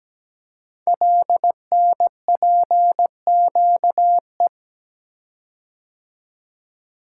Morse code, heard as a keyed tone, about 17 wpm, 700 Hz. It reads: LNPQE
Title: Code Groups - 5 Letters - 17wpm